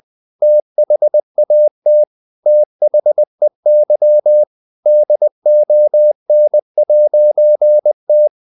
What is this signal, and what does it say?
Morse code, heard as a keyed tone, about 20 wpm, 600 Hz. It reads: THAT THEY DON'T
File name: CW / Practice / EnglishQuotations0320